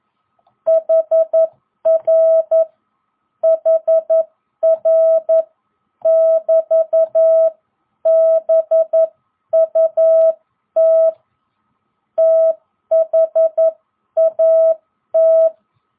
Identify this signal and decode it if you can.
Morse code, heard as a keyed tone, about 11 wpm, spaced as for 9 wpm, 660 Hz. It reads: HR HR<BT>BUT THAT